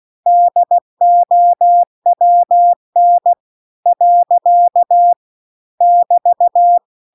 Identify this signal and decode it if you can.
Morse code, heard as a keyed tone, about 16 wpm, 700 Hz. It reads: DOWN . <BT>